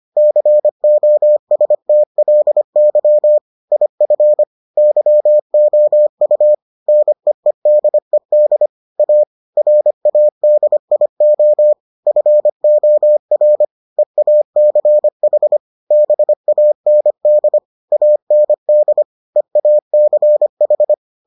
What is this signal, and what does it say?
Morse code, heard as a keyed tone, about 25 wpm, 600 Hz. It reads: COSTLY IF YOU NEEDED A RADIO FOR EACH BAND AND EACH